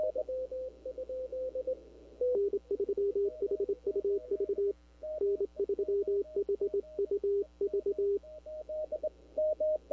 Morse code, heard as a keyed tone, about 26 wpm, 400 Hz, with radio noise. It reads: N3HUV N3HUV
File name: CW / 2005-k5tr arrl cw dx -21:00